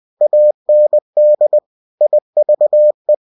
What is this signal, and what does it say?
Morse code, heard as a keyed tone, about 20 wpm, 600 Hz. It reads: AND IVE
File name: cw-books / winnie-the-pooh / winnie-20wpm-page-0054